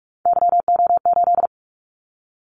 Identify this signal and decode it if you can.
Morse code, heard as a keyed tone, about 46 wpm, 700 Hz. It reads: YO8
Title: Callsign Prefixes for Top 25 DXCC - 45wpm